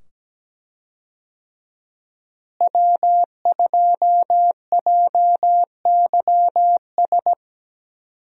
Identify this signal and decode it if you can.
Morse code, heard as a keyed tone, about 17 wpm, 700 Hz. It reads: W2JYS
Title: USA Call Signs - 5 Characters - Repeated 2x - 17wpm